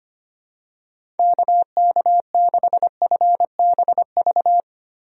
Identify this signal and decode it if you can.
Morse code, heard as a keyed tone, about 25 wpm, 700 Hz. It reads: KK6FBV
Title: USA Call Signs - 6 Characters - Repeated 3x - 25wpm